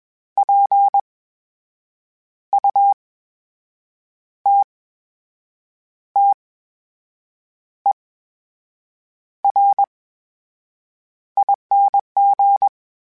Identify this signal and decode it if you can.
Morse code, heard as a keyed tone, about 21 wpm, spaced as for 6 wpm, 800 Hz. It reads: PUTTERING
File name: Special INGs-mp3s / Special INGs-mp3s / VerySlowToFaster Ing @20